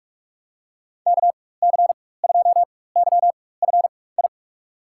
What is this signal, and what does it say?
Morse code, heard as a keyed tone, about 46 wpm, spaced as for 21 wpm, 700 Hz. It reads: KC2YFI
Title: USA Call Signs - 6 Characters - 4x Character Spacing - 45wpm